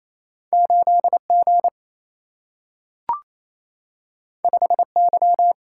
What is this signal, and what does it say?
Morse code, heard as a keyed tone, about 28 wpm, 700 Hz. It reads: 8 G 5 Y
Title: Single-Letter Single-Number 28wpm